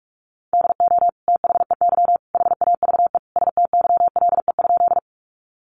Morse code, heard as a keyed tone, about 44 wpm, 700 Hz. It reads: DO THEY HAVE STYLE?
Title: Sentences from Top 800 Words 45wpm